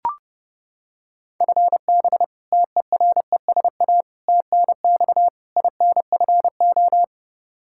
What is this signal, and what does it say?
Morse code, heard as a keyed tone, about 32 wpm, 700 Hz. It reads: FB TERESA TNX INFO
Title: Practice QSO - Exchange 3 - More Info - 30wpm